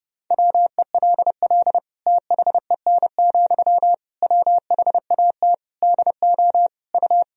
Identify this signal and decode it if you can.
Morse code, heard as a keyed tone, about 30 wpm, 700 Hz. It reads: WELL THEN, WHAT DO U
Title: Sentences from Top 200 Words and Abbreviations 30wpm